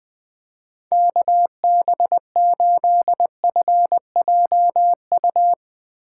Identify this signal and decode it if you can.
Morse code, heard as a keyed tone, about 20 wpm, 700 Hz. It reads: KB8FJU
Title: Speed Racing - USA 2x3 Call Signs 17wpm